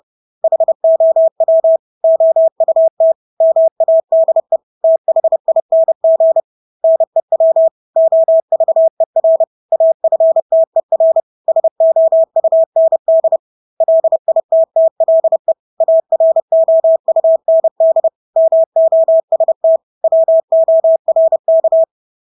Morse code, heard as a keyed tone, about 30 wpm, 650 Hz. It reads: HOW OUT MADE THING NEW OVER AFTER SOUND LITTLE AROUND MOST WORK
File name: Engelske ord 30-30 wpm